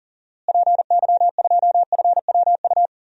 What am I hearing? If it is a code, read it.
Morse code, heard as a keyed tone, about 40 wpm, 700 Hz. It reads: PY2FWU
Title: DXCC-100 Call Signs - 6 Characters - Repeated 2x - 40wpm